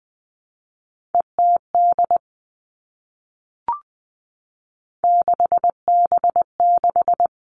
Morse code, heard as a keyed tone, about 20 wpm, 700 Hz. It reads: ETD 6B6
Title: All Cut Numbers - 3-Digit 20wpm